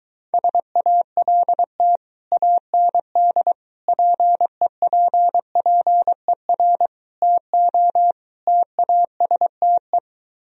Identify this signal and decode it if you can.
Morse code, heard as a keyed tone, about 22 wpm, 700 Hz. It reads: SALT AND PEPPER TO TASTE